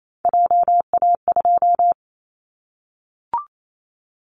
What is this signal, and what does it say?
Morse code, heard as a keyed tone, about 28 wpm, 700 Hz. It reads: JA2
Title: Callsign Prefixes for Top 25 DXCC - 28wpm